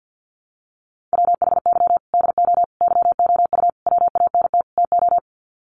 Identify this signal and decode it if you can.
Morse code, heard as a keyed tone, about 50 wpm, 700 Hz. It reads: WHY DO YOU WANT TO
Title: Sets of 5 Words Encore2 50wpm